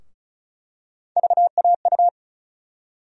Morse code, heard as a keyed tone, about 36 wpm, 700 Hz. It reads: VAU
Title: USA Call Sign Postfixes - 3 Letters 35wpm